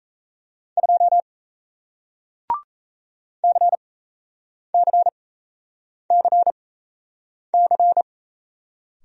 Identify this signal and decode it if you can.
Morse code, heard as a keyed tone, about 38 wpm, spaced as for 8 wpm, 700 Hz. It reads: 2 CCCC